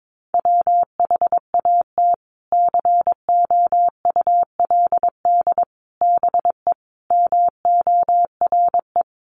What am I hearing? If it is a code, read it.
Morse code, heard as a keyed tone, about 22 wpm, 700 Hz. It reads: WHAT COULD BE MORE
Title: Sentences from Top 1000 Words 22wpm